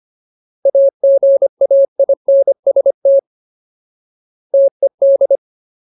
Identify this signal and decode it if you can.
Morse code, heard as a keyed tone, about 25 wpm, 550 Hz. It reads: AGAINST TED
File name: Ultimate Word List / 25WPM Ultimate